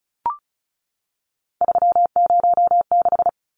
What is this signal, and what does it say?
Morse code, heard as a keyed tone, about 34 wpm, 700 Hz. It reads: 306
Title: Numbers - 3-Digit 35wpm